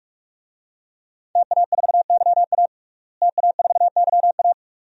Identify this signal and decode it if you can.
Morse code, heard as a keyed tone, about 46 wpm, 700 Hz. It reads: TA4YA TA4YA
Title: DXCC-100 Call Signs - 5 Characters - Repeated 3x - 45wpm